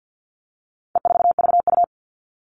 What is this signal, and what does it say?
Morse code, heard as a keyed tone, about 50 wpm, 700 Hz. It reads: E4VU
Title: All Cut Numbers - 4-Digit 50wpm